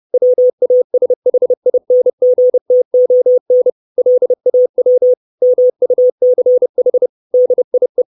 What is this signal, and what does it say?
Morse code, heard as a keyed tone, about 30 wpm, 500 Hz. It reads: WASHINGTON LAW MUCH DIE
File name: 632 Words 30WPM